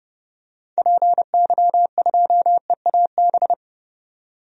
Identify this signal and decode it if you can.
Morse code, heard as a keyed tone, about 29 wpm, 690 Hz. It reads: PY2EAB
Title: DXCC-100 Call Signs - 6 Characters - 30wpm